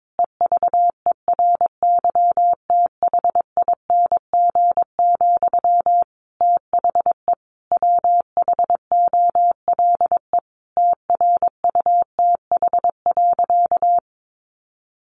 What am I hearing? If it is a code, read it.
Morse code, heard as a keyed tone, about 22 wpm, 700 Hz. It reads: EVERYTHING, THE WHOLE TRUTH.
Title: Sentences from Top 800 Words 22wpm